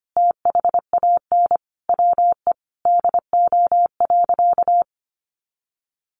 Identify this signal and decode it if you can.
Morse code, heard as a keyed tone, about 25 wpm, 700 Hz. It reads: THAN WE DO.
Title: Sentences from Top 100 Words 25wpm